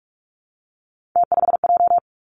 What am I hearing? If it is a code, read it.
Morse code, heard as a keyed tone, about 46 wpm, 700 Hz. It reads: T5J
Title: Three Challenging Letter-Number Combinations 45wpm